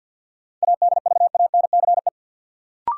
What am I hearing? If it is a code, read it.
Morse code, heard as a keyed tone, about 46 wpm, 700 Hz. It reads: ADVANCE
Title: Sentences from Top 1400 Words - New Word Drill - 50wpm